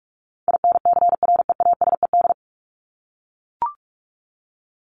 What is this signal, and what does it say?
Morse code, heard as a keyed tone, about 44 wpm, 700 Hz. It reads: INCREASED
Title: ED Suffix - Top 100 Words with Base Words 45wpm